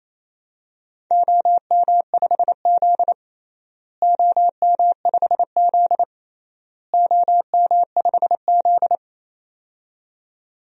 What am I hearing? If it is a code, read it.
Morse code, heard as a keyed tone, about 27 wpm, 700 Hz. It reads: OM5Z OM5Z OM5Z
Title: DXCC-100 Call Signs - 4 Characters - Repeated 3x - 28wpm